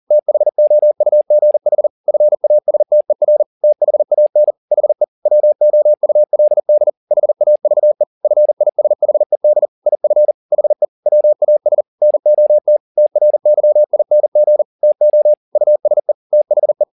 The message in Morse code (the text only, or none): THOUGH FASTER THAN HE WOULD HAVE FISHED IF HE WAS NOT TRYING TO USE THE